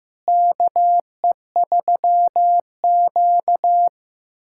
Morse code, heard as a keyed tone, about 15 wpm, 700 Hz. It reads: KE3Q
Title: USA Callsign DE Callsign - 4 Characters - 15wpm